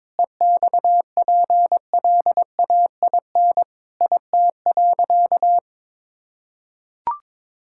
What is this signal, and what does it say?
Morse code, heard as a keyed tone, about 22 wpm, 700 Hz. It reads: EXPLAIN IT.
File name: practice / 22wpm / Sentences with 8-Word Phrases - 22wpm